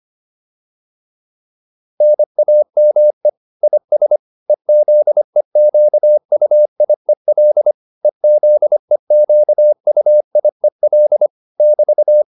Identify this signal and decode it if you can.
Morse code, heard as a keyed tone, about 25 wpm, 600 Hz. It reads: NAME IS EZEQUIEL EZEQUIEL <BT>